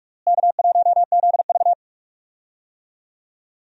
Morse code, heard as a keyed tone, about 43 wpm, 700 Hz. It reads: K1ZV
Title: USA Call Signs - 4 Characters - Repeated 2x - 45wpm